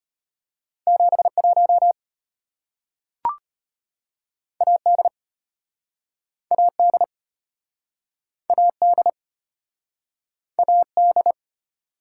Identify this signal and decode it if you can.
Morse code, heard as a keyed tone, about 33 wpm, 700 Hz. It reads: 71 AD AD AD AD